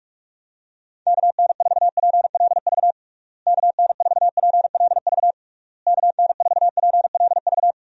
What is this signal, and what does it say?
Morse code, heard as a keyed tone, about 46 wpm, 700 Hz. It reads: KN4PLV KN4PLV KN4PLV
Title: USA Call Signs - 6 Characters - Repeated 3x - 45wpm